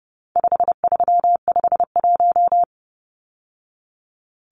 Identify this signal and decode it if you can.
Morse code, heard as a keyed tone, about 30 wpm, 700 Hz. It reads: 5351
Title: Numbers - 4-Digit 30wpm